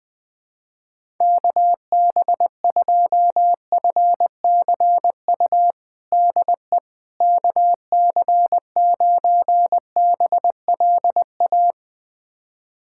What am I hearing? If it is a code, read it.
Morse code, heard as a keyed tone, about 20 wpm, 700 Hz. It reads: KB2FCU DE KC9BLA
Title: USA Callsign DE Callsign 20wpm